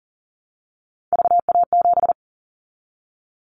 Morse code, heard as a keyed tone, about 40 wpm, 700 Hz. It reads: VA7
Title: Callsign Prefixes for Top 25 DXCC - 40wpm